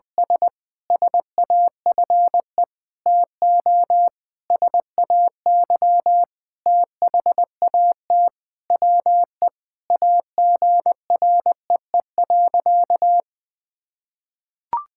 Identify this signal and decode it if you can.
Morse code, heard as a keyed tone, about 21 wpm, 700 Hz. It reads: S SAFE TO SAY THAT WE AGREE.
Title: Sentences with 8-Word Phrases - 20wpm